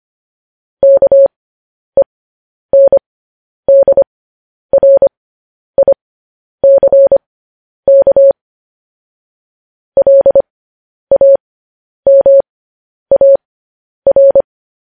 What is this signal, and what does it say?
Morse code, heard as a keyed tone, about 25 wpm, spaced as for 10 wpm, 550 Hz. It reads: KENDRICK LAMAR
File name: podcast / News headlines-1x-25x10